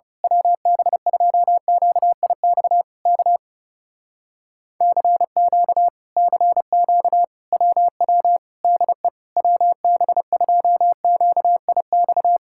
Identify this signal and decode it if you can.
Morse code, heard as a keyed tone, about 29 wpm, 700 Hz. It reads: WB2QIX K CQ CQ WW DE WB2QIX
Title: Speed-Racing Common QSO Phrases Encore - 30wpm